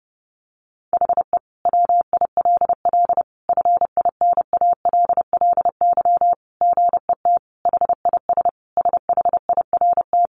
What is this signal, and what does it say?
Morse code, heard as a keyed tone, about 30 wpm, 700 Hz. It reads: HE WILL FINALLY GET HIS SHIRT
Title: Additional Easy Phrases 30wpm